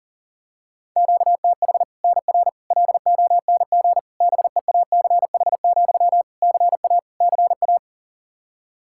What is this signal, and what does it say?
Morse code, heard as a keyed tone, about 37 wpm, 700 Hz. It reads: QTH NR LONG BEACH, CA CA
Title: K1USN SST QSO Practice 40wpm